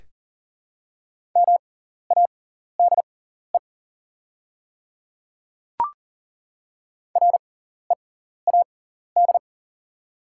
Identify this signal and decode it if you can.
Morse code, heard as a keyed tone, about 40 wpm, spaced as for 14 wpm, 690 Hz. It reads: MADE READ
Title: 6x Character Spacing - Top 300 Words - 40wpm